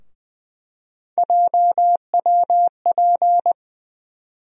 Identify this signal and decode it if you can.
Morse code, heard as a keyed tone, about 20 wpm, 700 Hz. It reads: JWP
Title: USA Call Sign Postfixes - 3 Letters 20wpm